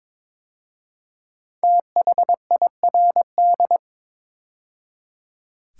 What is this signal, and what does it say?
Morse code, heard as a keyed tone, about 22 wpm, 700 Hz. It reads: THIRD
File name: 5 Letter Words - First 3 Letters Given - 22wpm